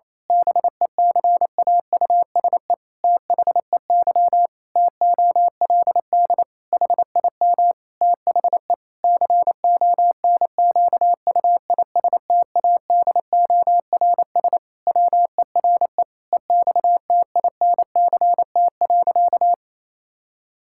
Morse code, heard as a keyed tone, about 28 wpm, 700 Hz. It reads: BECAUSE THEY TOLD HIM THE CONQUISTADORS WERE EXTINCT.
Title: Pangram Challenge v2 - 28wpm